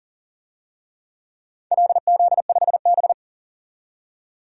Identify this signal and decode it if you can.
Morse code, heard as a keyed tone, about 42 wpm, 700 Hz. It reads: LZ5B
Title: DXCC-100 Call Signs - 4 Characters - 40wpm